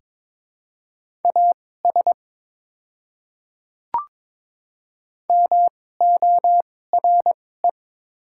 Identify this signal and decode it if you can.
Morse code, heard as a keyed tone, about 22 wpm, spaced as for 16 wpm, 700 Hz. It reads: AS MORE
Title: 2x Character Spacing - Top 100 Words - 22wpm